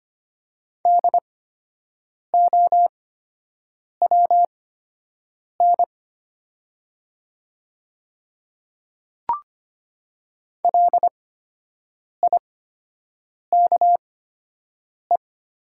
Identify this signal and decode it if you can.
Morse code, heard as a keyed tone, about 25 wpm, spaced as for 7 wpm, 700 Hz. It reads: DOWN LIKE
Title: 8x Character Spacing - Top 200 Words - 25wpm